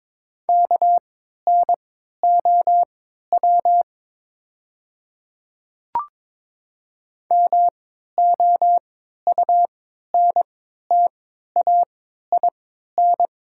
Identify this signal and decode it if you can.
Morse code, heard as a keyed tone, about 22 wpm, spaced as for 12 wpm, 700 Hz. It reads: KNOW MOUNTAIN
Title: 3x Character Spacing - Top 300 Words - 22wpm